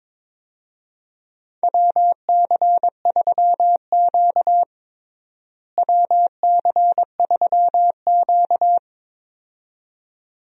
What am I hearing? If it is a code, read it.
Morse code, heard as a keyed tone, about 22 wpm, 700 Hz. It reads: WC3Q WC3Q
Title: USA Call Signs - 4 Characters - Repeated 2x - 22wpm